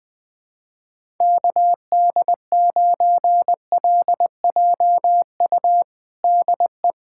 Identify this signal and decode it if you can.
Morse code, heard as a keyed tone, about 20 wpm, 700 Hz. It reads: KD9LJU DE
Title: USA Callsign DE Callsign 20wpm